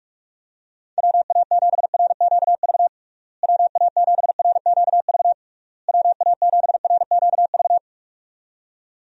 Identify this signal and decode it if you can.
Morse code, heard as a keyed tone, about 41 wpm, 700 Hz. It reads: WA7RQV WA7RQV WA7RQV
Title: USA Call Signs - 6 Characters - Repeated 3x - 45wpm